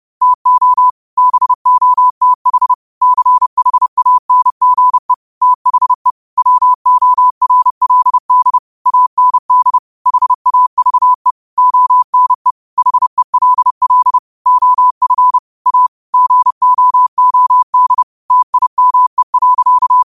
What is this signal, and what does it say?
Morse code, heard as a keyed tone, about 30 wpm, 1000 Hz. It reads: TO BOTH CHANGE THE WORLD AND HAVE ONE HELL OF A GOOD TIME.
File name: qotdfiles / 30wpm